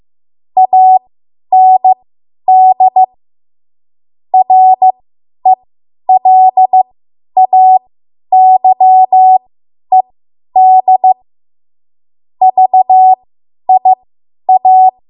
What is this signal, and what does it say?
Morse code, heard as a keyed tone, about 15 wpm, spaced as for 10 wpm, 750 Hz. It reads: AND RELAYED VIA
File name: Morse / 10WPM